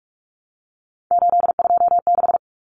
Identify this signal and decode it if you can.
Morse code, heard as a keyed tone, about 44 wpm, 700 Hz. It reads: 826
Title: Common Cut Numbers - 3-Digit 45wpm